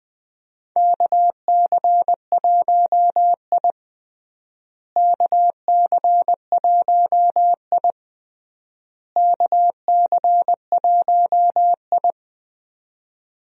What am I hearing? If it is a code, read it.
Morse code, heard as a keyed tone, about 21 wpm, 700 Hz. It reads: KC1I KC1I KC1I
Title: USA Call Signs - 4 Characters - Repeated 3x - 20wpm